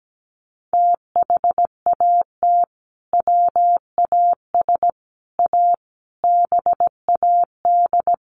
Morse code, heard as a keyed tone, about 17 wpm, 700 Hz. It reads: THAT WAS A BAD
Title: Sentences from Top 800 Words 17wpm